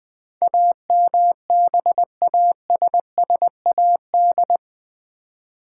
Morse code, heard as a keyed tone, about 20 wpm, 700 Hz. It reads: AMBASSAD
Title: Build A Word - 4 Letters At A Time - 10-Letter Words - 20wpm